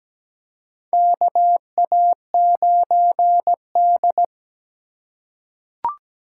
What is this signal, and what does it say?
Morse code, heard as a keyed tone, about 17 wpm, 700 Hz. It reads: KA9D
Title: USA Call Signs - 4 Characters - 8x Character Spacing - 17wpm